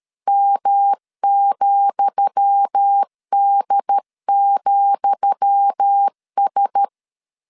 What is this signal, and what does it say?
Morse code, heard as a keyed tone, about 13 wpm, 800 Hz. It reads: M,D,S
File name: LEZIONI / 10 SIGNS / 10D COMA 12wpm 2544